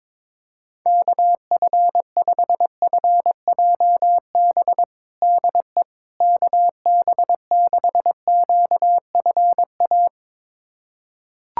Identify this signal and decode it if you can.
Morse code, heard as a keyed tone, about 22 wpm, 700 Hz. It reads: KF5FJB DE KB6QFA
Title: USA Callsign DE Callsign - 6 Characters - 22wpm